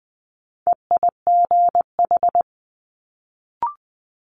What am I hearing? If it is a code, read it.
Morse code, heard as a keyed tone, about 20 wpm, 700 Hz. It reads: EIGH
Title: Multi-letter Phonemes Quick Fire 20wpm